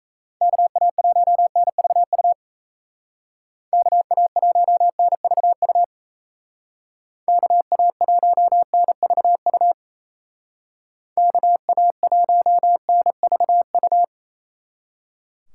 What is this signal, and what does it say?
Morse code, heard as a keyed tone, about 36 wpm, 700 Hz. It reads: KA1NVU KA1NVU KA1NVU KA1NVU